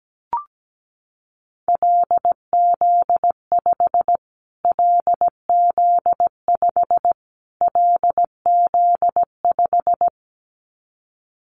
Morse code, heard as a keyed tone, about 17 wpm, 700 Hz. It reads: LZ5 LZ5 LZ5
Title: Callsign Prefixes for Top 25 DXCC - Repeated 3x - 17wpm